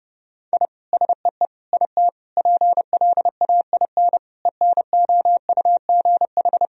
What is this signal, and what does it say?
Morse code, heard as a keyed tone, about 30 wpm, 700 Hz. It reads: I SEE IT PLAIN ENOUGH
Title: Sentences from Top 1000 Words - Review Part 2 - 30wpm